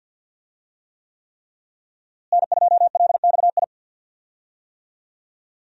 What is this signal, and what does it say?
Morse code, heard as a keyed tone, about 46 wpm, 700 Hz. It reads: N2LXI
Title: USA Call Signs - 5 Characters - Repeated 3x - 50wpm